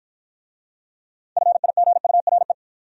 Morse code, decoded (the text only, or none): FIGURE